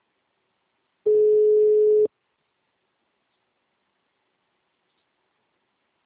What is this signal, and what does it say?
Ringback tone (Europe, 425 Hz).
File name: out- -2011- - - .10458